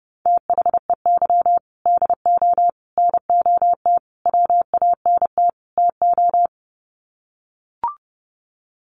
Morse code, heard as a keyed tone, about 30 wpm, 700 Hz. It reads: THEY DO NOT WANT TO
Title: Sets of 5 Words 30wpm